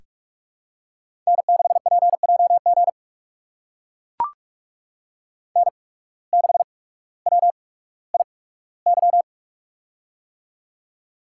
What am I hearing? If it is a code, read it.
Morse code, heard as a keyed tone, about 44 wpm, spaced as for 19 wpm, 700 Hz. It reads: N6PJG N6WIY
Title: USA Call Signs - 5 Characters - 8x Character Spacing - 45wpm